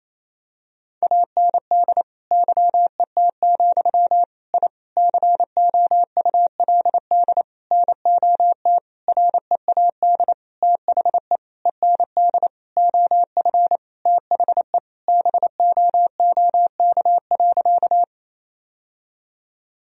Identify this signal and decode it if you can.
Morse code, heard as a keyed tone, about 28 wpm, 700 Hz. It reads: AND YET, I COULD NOT READ THE END OF THE BOOK.
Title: Sentences from Top 1000 Words - Review Part 1 - 28wpm